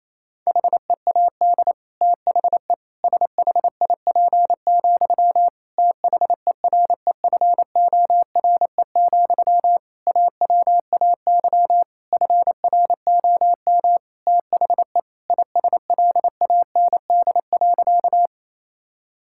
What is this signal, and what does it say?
Morse code, heard as a keyed tone, about 28 wpm, 700 Hz. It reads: HEAD THE SHIP, THEREFORE, AWAY FROM THE ISLAND.
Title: Sentences from Top 1000 Words - Review Part 5 - 28wpm